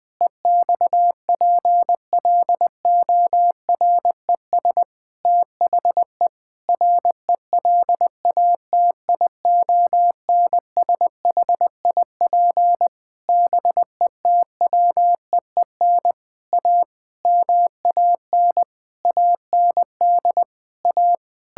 Morse code, heard as a keyed tone, about 20 wpm, 700 Hz. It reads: EXPLORES THE RELATIONSHIP BETWEEN A MAN AND A
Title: Sentences with 6-Word Phrases - Spoken Beginning - 20wpm